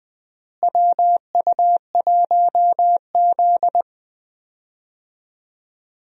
Morse code, heard as a keyed tone, about 20 wpm, 700 Hz. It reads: WU1Z
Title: USA Call Signs - 4 Characters - Repeated 3x - 20wpm